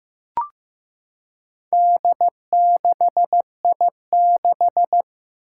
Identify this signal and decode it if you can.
Morse code, heard as a keyed tone, about 15 wpm, 700 Hz. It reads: D6I6
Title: Four Challenging Letter-Number Combinations 15wpm